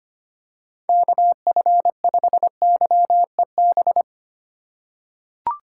Morse code, heard as a keyed tone, about 25 wpm, 700 Hz. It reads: KF5YEB
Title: USA Call Signs - 6 Characters - 3x Character Spacing - 25wpm